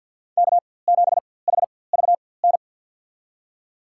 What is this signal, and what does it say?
Morse code, heard as a keyed tone, about 49 wpm, spaced as for 23 wpm, 700 Hz. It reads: K7HVN
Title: USA Call Signs - 5 Characters - 4x Character Spacing - 50wpm